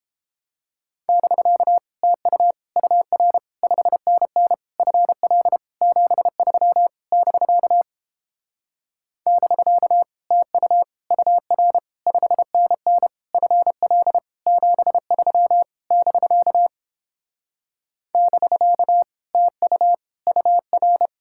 Morse code, heard as a keyed tone, about 31 wpm, 700 Hz. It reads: <BK> TU UR 5NN FL 73 <BK> <BK> TU UR 5NN FL 73 <BK> <BK> TU UR